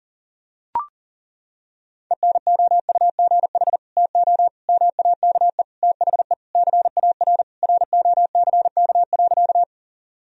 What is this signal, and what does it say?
Morse code, heard as a keyed tone, about 40 wpm, 700 Hz. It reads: ENOUGH TO MAKE THE CAR ROCK.
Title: Sentences from Top 1000 Words - Review Part 1 - 40wpm